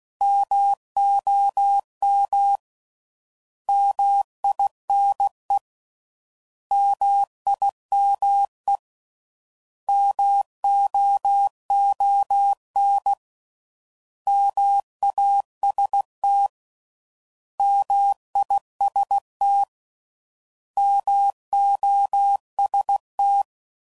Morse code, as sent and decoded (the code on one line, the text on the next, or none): -- --- -- / -- .. -. / -- .. -- . / -- --- --- -. / -- .- ... - / -- .. ... - / -- --- ... -
MOM MINE MIME MOON MAST MIST MOST